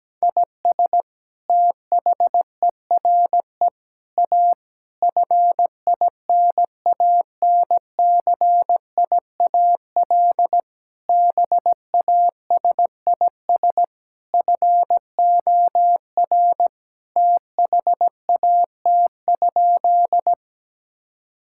Morse code, heard as a keyed tone, about 17 wpm, 700 Hz. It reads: IS THERE A FINANCIAL BASIS FOR THAT?
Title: Sentences from Top 1000 Words - Review Part 5 - 17wpm